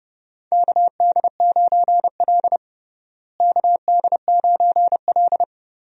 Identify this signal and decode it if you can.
Morse code, heard as a keyed tone, about 30 wpm, 700 Hz. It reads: KD9L KD9L
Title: USA Call Signs - 4 Characters - Repeated 3x - 30wpm